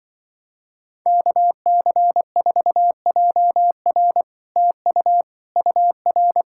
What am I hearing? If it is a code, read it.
Morse code, heard as a keyed tone, about 24 wpm, 700 Hz. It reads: KC4JR TU UR
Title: Speed-Racing Common QSO Phrases - 20wpm